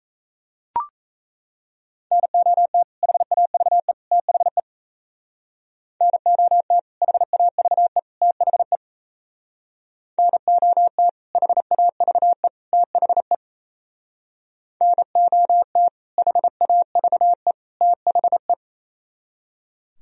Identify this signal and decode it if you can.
Morse code, heard as a keyed tone, about 37 wpm, 700 Hz. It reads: NOT HAVE THE NOT HAVE THE NOT HAVE THE NOT HAVE THE